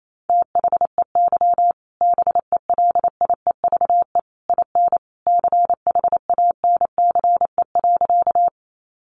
Morse code, heard as a keyed tone, about 28 wpm, 700 Hz. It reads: THEY BELIEVE IN CHANCE.
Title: Sentences from Top 1000 Words 28wpm